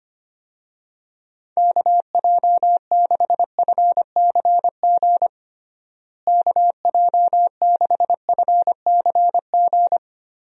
Morse code, heard as a keyed tone, about 25 wpm, 700 Hz. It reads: KJ6FCG KJ6FCG
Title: USA Call Signs - 6 Characters - Repeated 3x - 25wpm